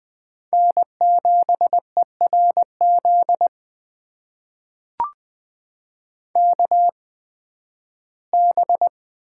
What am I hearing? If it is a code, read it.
Morse code, heard as a keyed tone, about 20 wpm, spaced as for 13 wpm, 700 Hz. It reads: N7ERZ KB
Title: USA Call Signs - 5 Characters - 8x Character Spacing - 20wpm